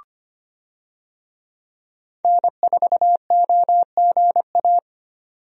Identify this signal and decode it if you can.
Morse code, heard as a keyed tone, about 25 wpm, 700 Hz. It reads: N4OGA
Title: USA Call Signs - 5 Characters - 25wpm